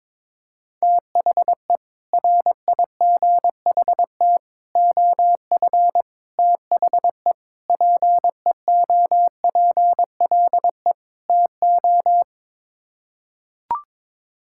Morse code, heard as a keyed tone, about 22 wpm, 700 Hz. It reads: THE RIGHT OF THE PEOPLE TO
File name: practice / 22wpm / Sets of 6 Words - 22wpm